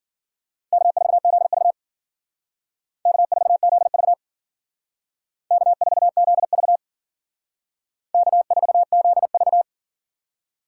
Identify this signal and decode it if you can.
Morse code, heard as a keyed tone, about 51 wpm, 700 Hz. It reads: K4ZV K4ZV K4ZV K4ZV